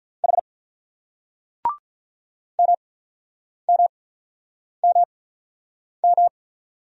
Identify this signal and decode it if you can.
Morse code, heard as a keyed tone, about 48 wpm, 700 Hz. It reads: H M M M M